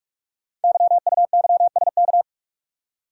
Morse code, heard as a keyed tone, about 44 wpm, 700 Hz. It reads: YUYSK
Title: Code Groups - 5 Letters - 45wpm